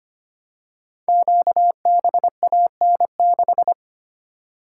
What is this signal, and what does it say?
Morse code, heard as a keyed tone, about 25 wpm, 700 Hz. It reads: QBAN6
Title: Code Groups - 5 Letter:Numbers - 25wpm